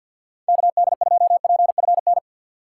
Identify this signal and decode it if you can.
Morse code, heard as a keyed tone, about 53 wpm, 700 Hz. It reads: KD2PFN
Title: USA Call Signs - 4x Character Spacing - 50wpm